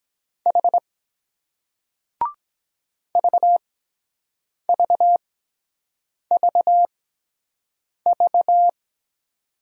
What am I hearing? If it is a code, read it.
Morse code, heard as a keyed tone, about 23 wpm, spaced as for 7 wpm, 700 Hz. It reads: H VVVV